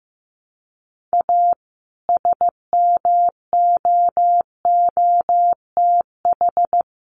Morse code, heard as a keyed tone, about 15 wpm, 700 Hz. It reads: A SMOOTH
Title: Sentences from Top 1400 Words 15wpm